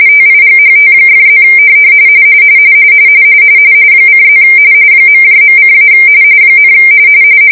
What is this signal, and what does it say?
RTTY, about 45.45 baud. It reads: TIGHT!+ :+ ..:- ,